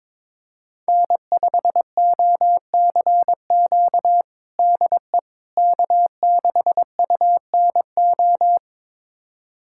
Morse code, heard as a keyed tone, about 22 wpm, 700 Hz. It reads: N5OCQ DE K6UNO
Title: USA Callsign DE Callsign - 5 Characters - 22wpm